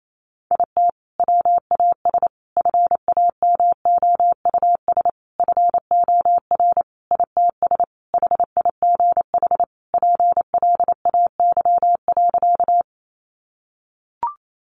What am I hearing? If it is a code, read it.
Morse code, heard as a keyed tone, about 28 wpm, 700 Hz. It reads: IT WAS FAMOUS FOR ITS HIGH PLAY.
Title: Sentences from Top 1500 Words 28wpm